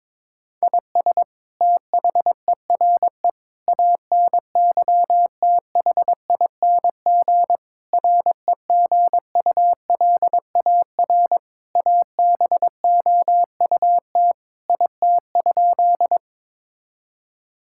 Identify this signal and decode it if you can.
Morse code, heard as a keyed tone, about 22 wpm, 700 Hz. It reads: IS THERE ANYTHING REGULAR ABOUT IT?
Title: Sentences from Top 1000 Words - Review Part 6 - 22wpm